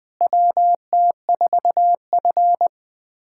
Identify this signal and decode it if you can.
Morse code, heard as a keyed tone, about 20 wpm, 700 Hz. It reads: WT4F
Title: USA Callsign DE Callsign - 4 Characters - 20wpm